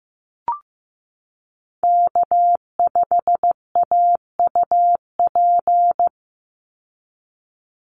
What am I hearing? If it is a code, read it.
Morse code, heard as a keyed tone, about 15 wpm, 700 Hz. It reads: K5AUP
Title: Call Signs Repeated 3x - 15wpm